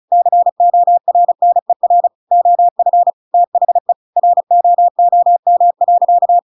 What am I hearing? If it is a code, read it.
Morse code, heard as a keyed tone, about 34 wpm, 700 Hz. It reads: CORNER OF THE ROOM.
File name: Pinocchio / Pinocchio ch7 35 35 ik1hge